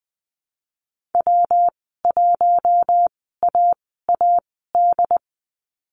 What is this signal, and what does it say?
Morse code, heard as a keyed tone, about 20 wpm, spaced as for 15 wpm, 700 Hz. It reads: W1AAD
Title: 193 Lesson 31g - Letter v - Call Signs 20wpm